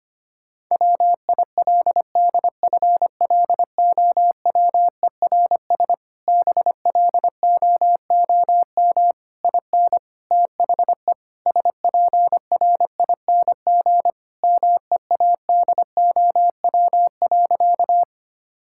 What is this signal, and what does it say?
Morse code, heard as a keyed tone, about 25 wpm, 700 Hz. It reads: WILDFLOWERS BLOOM IN THE SPRING MEADOW.
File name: practice / 25wpm / Difficult Sentences with Hints - 25wpm